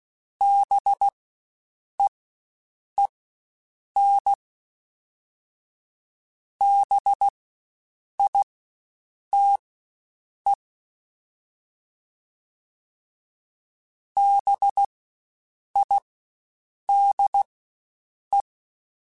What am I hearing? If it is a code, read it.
Morse code, heard as a keyed tone, about 16 wpm, spaced as for 7 wpm, 800 Hz. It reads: BEEN BITE BIDE
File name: Spaced Alphas / Spaced Alphas mp3s / Bs sp @15